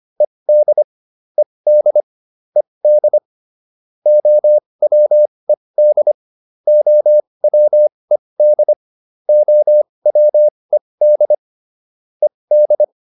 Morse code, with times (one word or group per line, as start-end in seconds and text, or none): 0.2-0.8 ED
1.4-2.0 ED
2.6-3.2 ED
4.1-6.1 OWED
6.7-8.7 OWED
9.3-11.3 OWED
12.2-12.8 ED